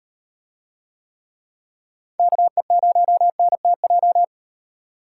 Morse code, heard as a keyed tone, about 38 wpm, 700 Hz. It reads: KE0NTJ